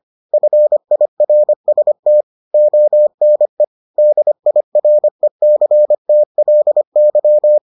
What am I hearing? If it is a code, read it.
Morse code, heard as a keyed tone, about 25 wpm, 600 Hz. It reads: FIRST ONE DIRECTLY